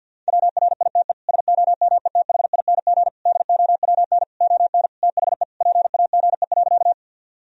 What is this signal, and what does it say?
Morse code, heard as a keyed tone, about 50 wpm, 700 Hz. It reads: WRITE SOMETHING DOWN ON THE PAGE.
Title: Sentences from Top 1000 Words - Review Part 1 - 50wpm